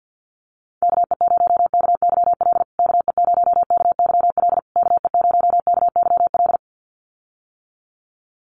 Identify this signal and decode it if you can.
Morse code, heard as a keyed tone, about 50 wpm, 700 Hz. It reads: KE0KYL KE0KYL KE0KYL
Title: USA Call Signs Repeated 3x 50wpm